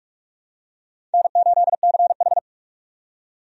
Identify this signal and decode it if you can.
Morse code, heard as a keyed tone, about 46 wpm, 700 Hz. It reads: N8CH
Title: USA Call Signs - 4 Characters - 45wpm